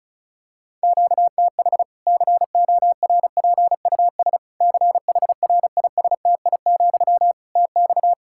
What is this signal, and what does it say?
Morse code, heard as a keyed tone, about 34 wpm, 700 Hz. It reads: QTH CORPUS CHRISTI, TX
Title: K1USN SST QSO Practice 35wpm